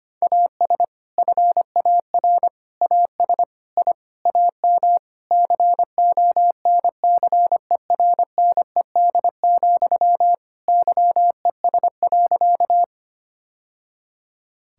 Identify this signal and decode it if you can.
Morse code, heard as a keyed tone, about 25 wpm, 700 Hz. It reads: AS FAR AS I AM CONCERNED, YES.
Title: Sentences from Top 1000 Words - Review Part 6 - 25wpm